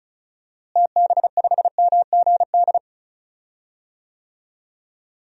Morse code, heard as a keyed tone, about 36 wpm, 700 Hz. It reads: TB5MGD
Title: DXCC-100 Call Signs - 6 Characters - Repeated 2x - 35wpm